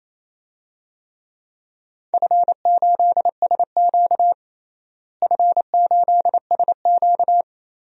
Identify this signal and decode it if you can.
Morse code, heard as a keyed tone, about 28 wpm, 700 Hz. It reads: F8SQ F8SQ
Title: DXCC-100 Call Signs - 4 Characters - Repeated 2x - 28wpm